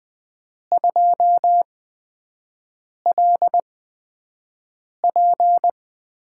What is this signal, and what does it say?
Morse code, heard as a keyed tone, about 20 wpm, spaced as for 5 wpm, 700 Hz. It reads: 2LP
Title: USA Call Signs - 6 Characters - 8x Character Spacing - 20wpm